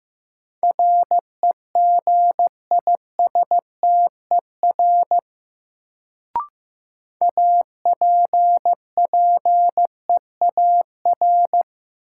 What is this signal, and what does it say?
Morse code, heard as a keyed tone, about 15 wpm, 700 Hz. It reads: REGISTER APPEAR
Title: Sentences from Top 1200 Words - New Word Drill Alt - 15wpm